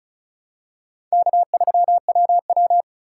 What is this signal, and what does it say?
Morse code, heard as a keyed tone, about 34 wpm, 700 Hz. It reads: K3WW
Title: USA Call Signs - 4 Characters - 4x Character Spacing - 35wpm